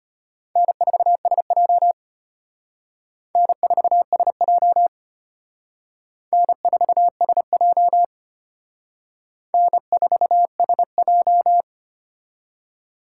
Morse code, heard as a keyed tone, about 33 wpm, 700 Hz. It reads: N4SJ N4SJ N4SJ N4SJ